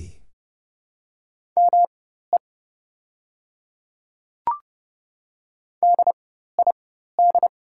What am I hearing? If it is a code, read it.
Morse code, heard as a keyed tone, about 30 wpm, spaced as for 14 wpm, 700 Hz. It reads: ME DID
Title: 4x Character Spacing - Top 200 Words - 30wpm